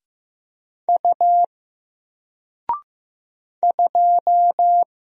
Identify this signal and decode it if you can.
Morse code, heard as a keyed tone, about 15 wpm, 700 Hz. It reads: U 2
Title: Single Letter-Number Alt 15wpm